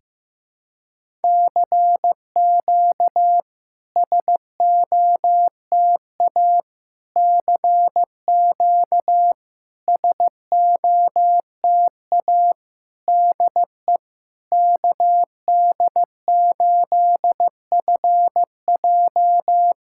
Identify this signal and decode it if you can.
Morse code, heard as a keyed tone, about 15 wpm, 700 Hz. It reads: CQ SOTA CQ SOTA DE KD8FJ
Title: SOTA QSO Practice 15wpm